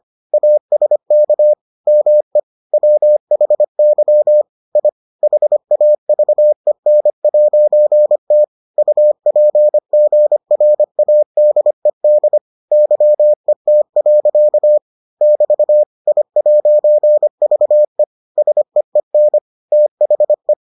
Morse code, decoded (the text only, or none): ASK ME WHY I HAVEN'T UPGRADED YET. <BT> I'VE SEEN THE